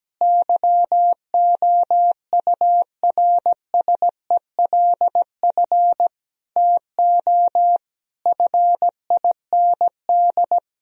Morse code, heard as a keyed tone, about 17 wpm, 700 Hz. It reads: YOURSELF TO FIND
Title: Sentences from Top 1000 Words - Review Part 4 - 17wpm